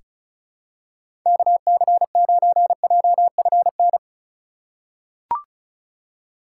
Morse code, heard as a keyed tone, about 34 wpm, 700 Hz. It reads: KC9JFN
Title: USA Call Signs - 6 Characters - 2x Character Spacing - 35wpm